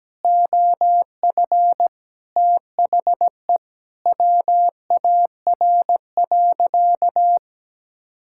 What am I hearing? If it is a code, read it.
Morse code, heard as a keyed tone, about 17 wpm, 700 Hz. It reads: OF THE WAR.
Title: Sentences from Top 1000 Words - Review Part 6 - 17wpm